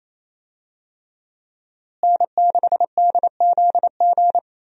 Morse code, heard as a keyed tone, about 28 wpm, 700 Hz. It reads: N6DZG
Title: USA Call Signs - 5 Characters - Repeated 3x - 28wpm